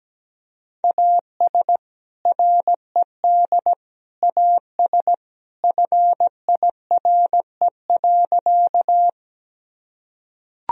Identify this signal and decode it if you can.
Morse code, heard as a keyed tone, about 17 wpm, 700 Hz. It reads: AS RED AS FIRE.
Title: Sentences from Top 1000 Words - Review Part 2 - 17wpm